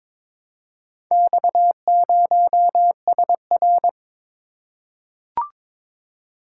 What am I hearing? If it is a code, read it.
Morse code, heard as a keyed tone, about 21 wpm, 700 Hz. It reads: X0SR
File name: practice / 22wpm / Four Letter-Number Combinations - 22wpm